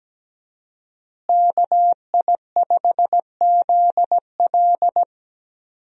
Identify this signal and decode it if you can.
Morse code, heard as a keyed tone, about 17 wpm, 700 Hz. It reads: KI5ZL
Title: USA Call Signs - 5 Characters - Repeated 2x - 17wpm